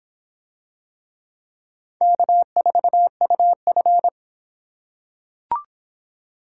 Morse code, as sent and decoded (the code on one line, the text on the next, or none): -.- ....- ..- ..-.
K4UF